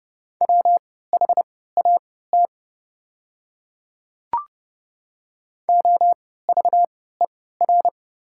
Morse code, heard as a keyed tone, about 30 wpm, spaced as for 17 wpm, 700 Hz. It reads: WHAT OVER
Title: 3x Character Spacing - Top 100 Words - 30wpm